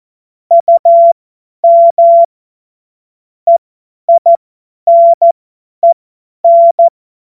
Morse code, heard as a keyed tone, about 14 wpm, spaced as for 10 wpm, 680 Hz. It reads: UM EINEN
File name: Morsekurs / Witze / Witz 13. April 2021, 50 F70